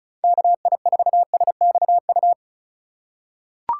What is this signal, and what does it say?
Morse code, heard as a keyed tone, about 36 wpm, 700 Hz. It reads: KI4SXU
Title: USA Call Signs - 6 Characters - 3x Character Spacing - 35wpm